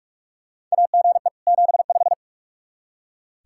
Morse code, heard as a keyed tone, about 45 wpm, 700 Hz. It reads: AGE 75
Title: K1USN SST QSO Practice 45wpm